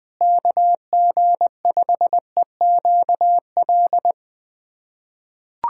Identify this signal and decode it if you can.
Morse code, heard as a keyed tone, about 20 wpm, 700 Hz. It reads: KG5EQL
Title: USA Callsign DE Callsign - 6 Characters - 20wpm